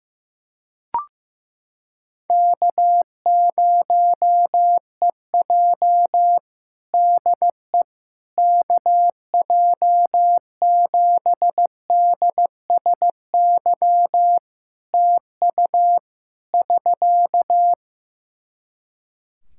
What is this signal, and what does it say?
Morse code, heard as a keyed tone, about 15 wpm, 700 Hz. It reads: K0EJ DE KJ7DSY TU <SK>
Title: Practice QSO - Exchange 4 - Ending - 15wpm